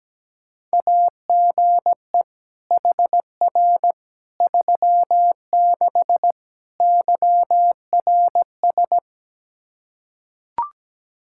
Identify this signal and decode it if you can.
Morse code, heard as a keyed tone, about 17 wpm, 700 Hz. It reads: AGE HR 36 YRS
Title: Practice QSO - Exchange 3 - More Info - 17wpm